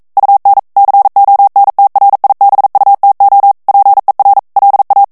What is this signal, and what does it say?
Morse code, heard as a keyed tone, about 42 wpm, 800 Hz. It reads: UN CONTRIBUTO PER LA
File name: ITALIEN texte clair 1 vitesse 40 mots minute